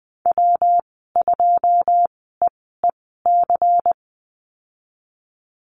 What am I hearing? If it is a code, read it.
Morse code, heard as a keyed tone, about 20 wpm, spaced as for 15 wpm, 700 Hz. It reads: W2EEC
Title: 186 Lesson 30g - Letter b - Call Signs 20wpm